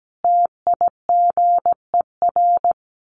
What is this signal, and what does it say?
Morse code, heard as a keyed tone, about 17 wpm, 700 Hz. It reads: TIGER
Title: A Sound Of Thunder 17wpm